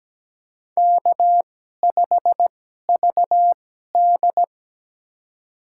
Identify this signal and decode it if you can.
Morse code, heard as a keyed tone, about 17 wpm, spaced as for 12 wpm, 700 Hz. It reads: K5VD
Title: USA Call Signs - 4 Characters - 2x Character Spacing - 17wpm